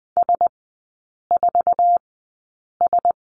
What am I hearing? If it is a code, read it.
Morse code, heard as a keyed tone, about 20 wpm, spaced as for 8 wpm, 700 Hz. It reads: S4S
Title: 047 Lesson 9e - Number 4 - Three Characters 20wpm